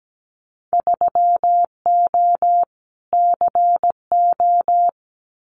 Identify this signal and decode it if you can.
Morse code, heard as a keyed tone, about 17 wpm, 700 Hz. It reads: 3O CO
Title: Winter Field Day Exchanges Encore 17wpm